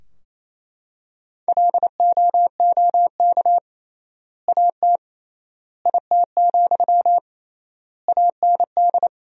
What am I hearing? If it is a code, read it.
Morse code, heard as a keyed tone, about 28 wpm, 700 Hz. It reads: LOOK AT IT, AND
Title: 3x Word-Spacing - Sentences from Top 200 Words - 28wpm